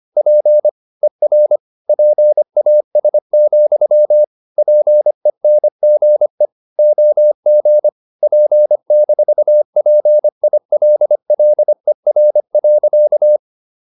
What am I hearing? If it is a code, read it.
Morse code, heard as a keyed tone, about 25 wpm, 600 Hz. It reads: P ER PAS, PENGE OG P-PILLER.
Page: Ferie 25 wpm